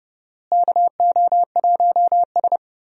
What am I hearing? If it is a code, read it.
Morse code, heard as a keyed tone, about 30 wpm, 700 Hz. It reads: KO1S
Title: USA Call Signs - 4 Characters - Repeated 3x - 30wpm